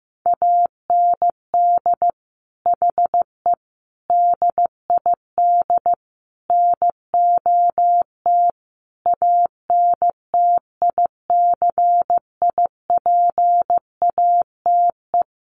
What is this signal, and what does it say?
Morse code, heard as a keyed tone, about 15 wpm, 700 Hz. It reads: AND HE DID NOT ANTICIPATE
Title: Sentences from Top 1500 Words 15wpm